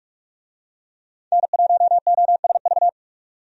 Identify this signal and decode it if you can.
Morse code, heard as a keyed tone, about 45 wpm, 700 Hz. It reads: N1OSV
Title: USA Call Signs - 5 Characters - 45wpm